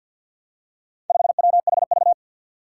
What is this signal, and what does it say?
Morse code, heard as a keyed tone, about 47 wpm, 700 Hz. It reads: 5WHV
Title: Four Challenging Letter-Number Combinations 50wpm